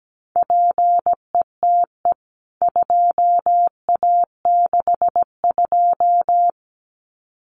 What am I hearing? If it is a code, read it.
Morse code, heard as a keyed tone, about 17 wpm, 700 Hz. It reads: PETE 2A62
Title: CWT Exchange 17wpm